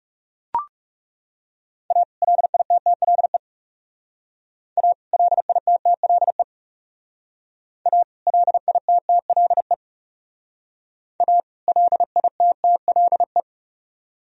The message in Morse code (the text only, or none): A LITTLE A LITTLE A LITTLE A LITTLE